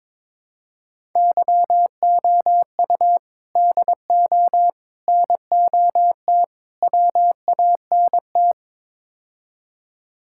Morse code, heard as a keyed tone, about 21 wpm, 700 Hz. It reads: YOU DO NOT WANT
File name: practice / 22wpm / Build A Phrase - 5-Word Phrases - 22wpm